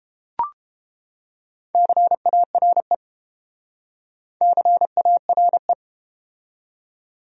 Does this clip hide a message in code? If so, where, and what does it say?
Yes: Morse code from 1.7 to 5.7 s, heard as a keyed tone, about 31 wpm, 700 Hz. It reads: CARE CARE